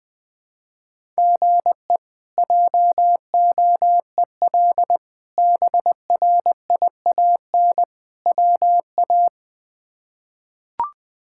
Morse code, heard as a keyed tone, about 20 wpm, 700 Hz. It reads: GE JOEL BRIAN WA
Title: K1USN SST QSO Practice - 20wpm